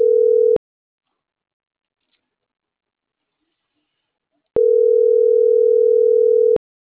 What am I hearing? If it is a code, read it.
Ringback tone (North America), 440 + 480 Hz.